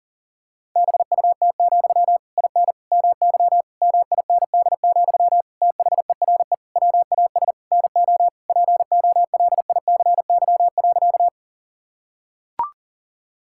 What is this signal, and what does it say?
Morse code, heard as a keyed tone, about 40 wpm, 700 Hz. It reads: BUT, IN MY MIND, THERE WAS NO POLICY.
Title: Sentences from Top 1000 Words - Review Part 5 - 40wpm